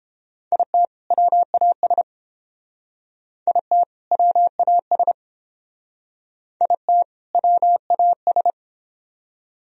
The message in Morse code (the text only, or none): IT WAS IT WAS IT WAS